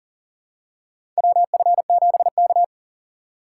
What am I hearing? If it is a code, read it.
Morse code, heard as a keyed tone, about 40 wpm, 700 Hz. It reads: WF7K
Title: USA Call Signs - 4 Characters - 3x Character Spacing - 40wpm